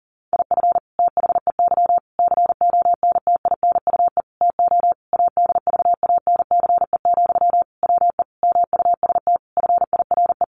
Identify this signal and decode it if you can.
Morse code, heard as a keyed tone, about 37 wpm, 700 Hz. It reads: IF THEY CONTINUE TO ADVANCE, WE MUST FIRE
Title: Sentences from Top 1400 Words 40wpm